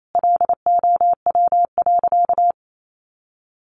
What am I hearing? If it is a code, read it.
Morse code, heard as a keyed tone, about 28 wpm, 700 Hz. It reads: LOW.
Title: Sentences from Top 500 Words 28wpm